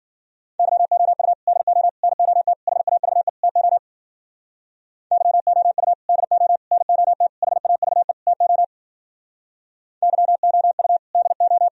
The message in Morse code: YOU DO NOT HAVE TO YOU DO NOT HAVE TO YOU DO

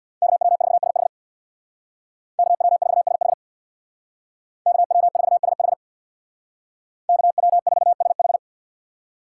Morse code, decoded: KW4SH KW4SH KW4SH KW4SH

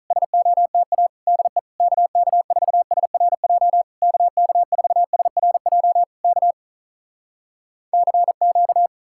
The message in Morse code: SOTA DE KK4SRJ KK4SRJ K CQ

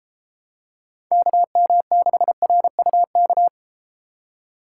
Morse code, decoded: KM6RUK

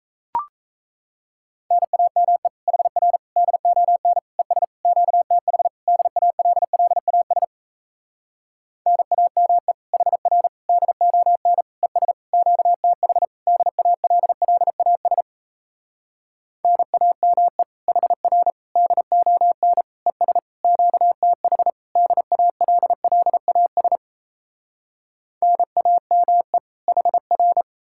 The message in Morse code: NAME HR DON ES QTH DALLAS NAME HR DON ES QTH DALLAS NAME HR DON ES QTH DALLAS NAME HR

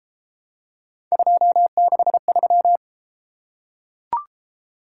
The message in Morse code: 263

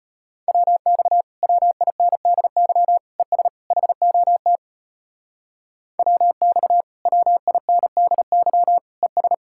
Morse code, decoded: WX WINDY ES HOT WX WINDY ES

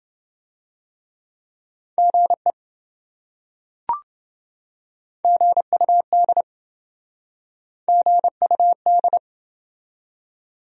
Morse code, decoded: GE GUD GUD

